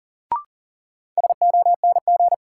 Morse code, heard as a keyed tone, about 40 wpm, 700 Hz. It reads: SONG
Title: Sentences from Top 400 Words - New Word Drill - 40wpm